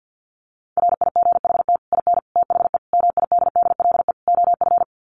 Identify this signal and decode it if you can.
Morse code, heard as a keyed tone, about 48 wpm, 700 Hz. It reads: RIGHT IN THE MIDDLE OF
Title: Sets of 5 Words 50wpm